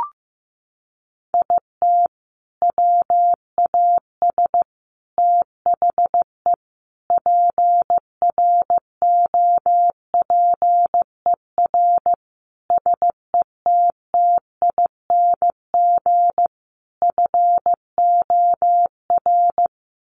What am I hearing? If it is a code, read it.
Morse code, heard as a keyed tone, about 15 wpm, 700 Hz. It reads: IT WAS THE PROPER SETTING FOR
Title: Sentences from Top 1300 Words 15wpm